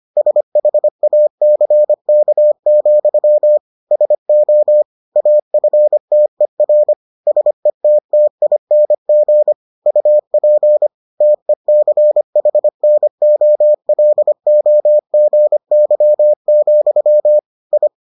Morse code, heard as a keyed tone, about 25 wpm, 600 Hz. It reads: SHACK, SO AFTER SETTING UP TECHNOLOGY, I